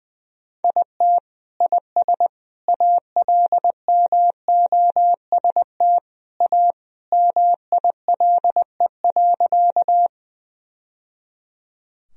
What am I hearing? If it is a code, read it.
Morse code, heard as a keyed tone, about 20 wpm, 700 Hz. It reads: IT IS ALMOST A MILE.
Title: Sentences from Top 1000 Words - Review Part 4 - 20wpm